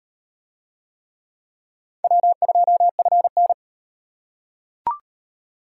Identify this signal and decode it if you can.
Morse code, heard as a keyed tone, about 35 wpm, 700 Hz. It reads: W2FN